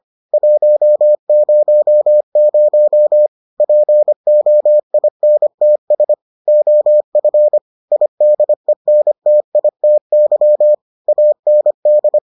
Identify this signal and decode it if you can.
Morse code, heard as a keyed tone, about 25 wpm, 600 Hz. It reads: 100 POINTS OF IDENTITY AND